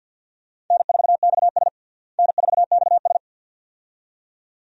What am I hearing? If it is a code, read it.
Morse code, heard as a keyed tone, about 50 wpm, 700 Hz. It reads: N4XS N4XS
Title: USA Call Signs - 4 Characters - Repeated 2x - 50wpm